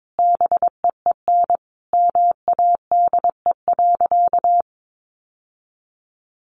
Morse code, heard as a keyed tone, about 22 wpm, 700 Hz. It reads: BEEN MADE.
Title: Sentences from Top 1400 Words 22wpm